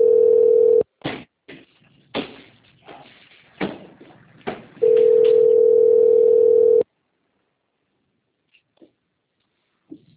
Ringback tone (North America), 440 + 480 Hz.